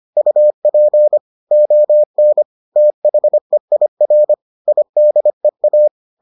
Morse code, heard as a keyed tone, about 25 wpm, 600 Hz. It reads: UP ON THEIR IDEA